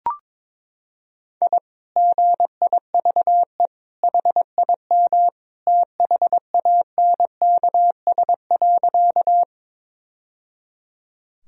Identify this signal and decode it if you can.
Morse code, heard as a keyed tone, about 22 wpm, 700 Hz. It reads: I GIVE HIM THANKS.
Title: Sentences from Top 1000 Words - Review Part 3 - 22wpm